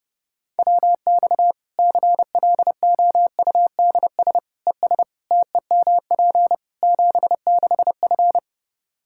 Morse code, heard as a keyed tone, about 30 wpm, 700 Hz. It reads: WX CLOUDS ES TEMP 76F
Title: K1USN SST QSO Practice 30wpm